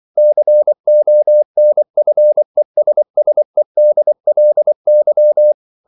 Morse code, heard as a keyed tone, about 24 wpm, 600 Hz. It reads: CONFESSEDLY
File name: CW / Practice / EnglishQuotations0424